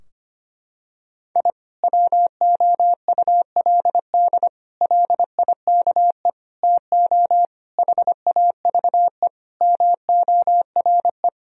Morse code, heard as a keyed tone, about 25 wpm, 700 Hz. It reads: I WOULD LIKE TO HAVE MORE
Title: Sentences from Top 1000 Words - Review Part 3 - 25wpm